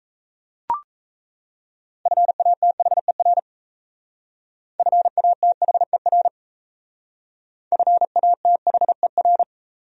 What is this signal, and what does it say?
Morse code, heard as a keyed tone, about 35 wpm, 700 Hz. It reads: FATHER FATHER FATHER